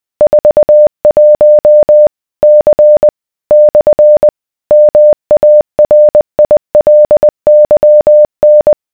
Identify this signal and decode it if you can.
Morse code, heard as a keyed tone, about 20 wpm, 600 Hz. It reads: 41 C / MARILYN